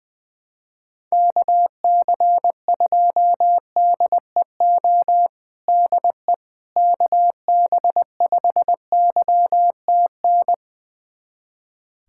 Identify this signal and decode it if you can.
Morse code, heard as a keyed tone, about 21 wpm, 700 Hz. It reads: KC2DEO DE KB5YTN
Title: USA Callsign DE Callsign 20wpm